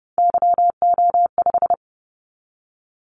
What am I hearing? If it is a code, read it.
Morse code, heard as a keyed tone, about 30 wpm, 700 Hz. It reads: YO5
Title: Callsign Prefixes for Top 25 DXCC - 30wpm